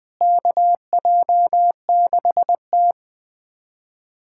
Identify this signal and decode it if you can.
Morse code, heard as a keyed tone, about 21 wpm, 700 Hz. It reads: KJ6T
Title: USA Callsign DE Callsign - 4 Characters - 20wpm